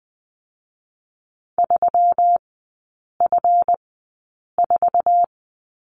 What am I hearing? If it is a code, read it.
Morse code, heard as a keyed tone, about 20 wpm, spaced as for 8 wpm, 700 Hz. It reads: 3F4
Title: 138 Lesson 23e - Letter f - Three Characters 20wpm